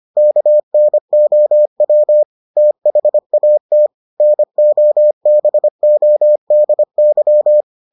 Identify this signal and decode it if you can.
Morse code, heard as a keyed tone, about 25 wpm, 600 Hz. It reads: KNOW THAT NOBODY